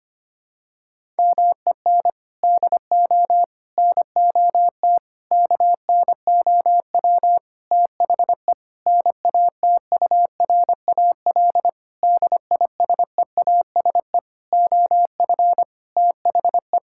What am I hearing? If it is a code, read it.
Morse code, heard as a keyed tone, about 25 wpm, 700 Hz. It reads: MEN DO NOT KNOW THE NATURAL DISEASE OF THE
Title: Sentences from Top 1000 Words - Review Part 6 - 25wpm